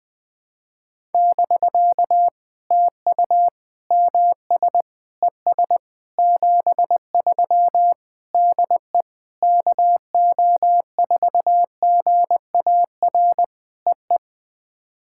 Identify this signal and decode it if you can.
Morse code, heard as a keyed tone, about 20 wpm, 700 Hz. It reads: <BK> TU MS ES 73 DE KO4GAR EE
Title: POTA QSO Practice - 20wpm